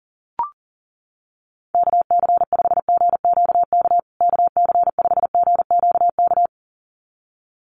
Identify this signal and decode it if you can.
Morse code, heard as a keyed tone, about 40 wpm, 700 Hz. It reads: KC5GQK KC5GQK
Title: USA Call Signs Repeated 2x 40wpm